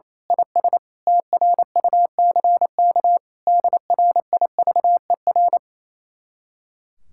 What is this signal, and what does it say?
Morse code, heard as a keyed tone, about 26 wpm, 700 Hz. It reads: IS TRUCK DRIVER
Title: Speed-Racing Common QSO Phrases - 28wpm